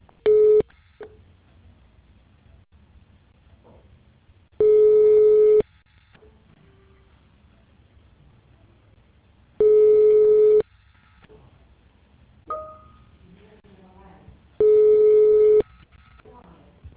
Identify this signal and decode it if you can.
Ringback tone (Europe, 425 Hz), with speech.